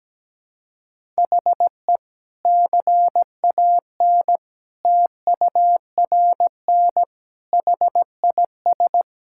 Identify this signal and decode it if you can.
Morse code, heard as a keyed tone, about 17 wpm, 700 Hz. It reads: HE CAN TURN HIS
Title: Sentences from Top 1600 Words 17wpm